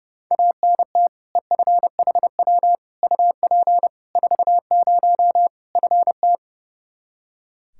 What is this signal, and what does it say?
Morse code, heard as a keyed tone, about 32 wpm, 700 Hz. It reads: ANT EFHW UP 40 FT
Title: K1USN SST QSO Practice 30wpm